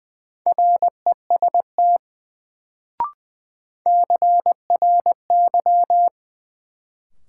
Morse code, heard as a keyed tone, about 20 wpm, 700 Hz. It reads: REST CRY
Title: Sentences from Top 500 Words - New Word Drill - 20wpm